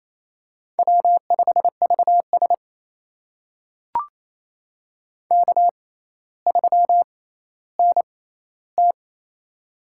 Morse code, heard as a keyed tone, about 28 wpm, spaced as for 14 wpm, 700 Hz. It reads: W5VS K3NT
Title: USA Call Signs - 4 Characters - 6x Character Spacing - 28wpm